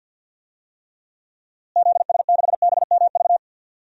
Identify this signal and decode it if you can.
Morse code, heard as a keyed tone, about 46 wpm, 700 Hz. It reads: ZS6BMV
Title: DXCC-100 Call Signs - 6 Characters - Repeated 3x - 50wpm